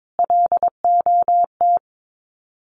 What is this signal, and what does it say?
Morse code, heard as a keyed tone, about 22 wpm, 700 Hz. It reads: LOT
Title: Sets of 5 Words Encore 22wpm